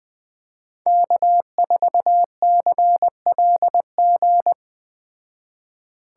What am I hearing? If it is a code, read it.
Morse code, heard as a keyed tone, about 20 wpm, 700 Hz. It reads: K4CLG
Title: Speed Racing - USA Call Signs 17wpm